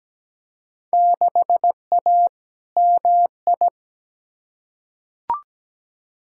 Morse code, heard as a keyed tone, about 17 wpm, 700 Hz. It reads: 6A MI
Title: Field Day Exchanges - 17wpm